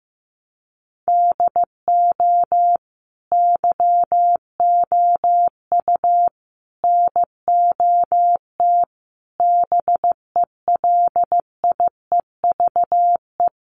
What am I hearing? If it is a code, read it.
Morse code, heard as a keyed tone, about 15 wpm, 700 Hz. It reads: DO YOU NOT BELIEVE
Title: Sentences from Top 1000 Words 15wpm